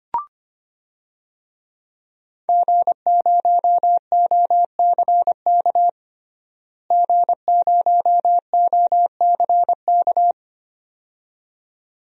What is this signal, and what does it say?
Morse code, heard as a keyed tone, about 25 wpm, 700 Hz. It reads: G0OCK G0OCK
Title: DXCC-100 Call Signs - 5 Characters - Repeated 2x - 25wpm